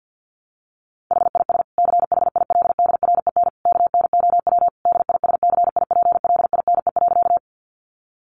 Morse code, heard as a keyed tone, about 50 wpm, 700 Hz. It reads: HIS CHILDREN KNOW DISCIPLINE.
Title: Sentences from Top 900 Words 50wpm